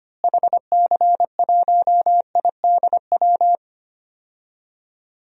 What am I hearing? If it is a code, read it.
Morse code, heard as a keyed tone, about 25 wpm, 700 Hz. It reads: HC1IDW
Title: DXCC-100 Call Signs - 6 Characters - Repeated 3x - 25wpm